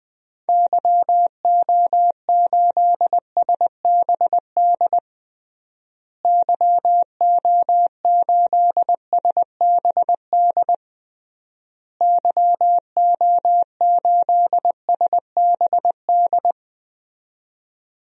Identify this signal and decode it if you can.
Morse code, heard as a keyed tone, about 20 wpm, 700 Hz. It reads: YO8SBD YO8SBD YO8SBD
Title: DXCC-100 Call Signs - 6 Characters - Repeated 3x - 20wpm